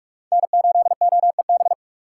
Morse code, heard as a keyed tone, about 45 wpm, 690 Hz. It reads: N8OEB
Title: USA Call Signs - 5 Characters - 6x Character Spacing - 45wpm